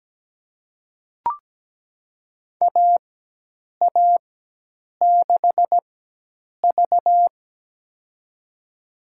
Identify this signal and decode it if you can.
Morse code, heard as a keyed tone, about 17 wpm, spaced as for 8 wpm, 700 Hz. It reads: AA6V
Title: USA Call Signs - 4 Characters - 4x Character Spacing - 17wpm